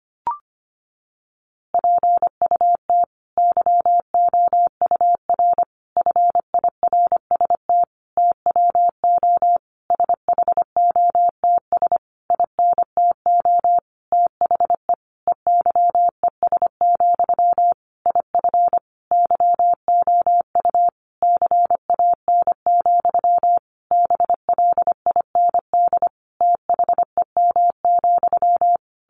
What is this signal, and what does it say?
Morse code, heard as a keyed tone, about 25 wpm, 700 Hz. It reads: PUT YOUR FIRST TWO SHOTS INTO THE EYES, IF YOU CAN, BLIND THEM,
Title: A Sound Of Thunder 25wpm